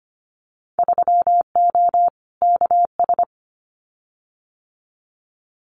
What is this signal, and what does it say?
Morse code, heard as a keyed tone, about 25 wpm, 700 Hz. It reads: 3O KS
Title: Winter Field Day Exchanges Encore 25wpm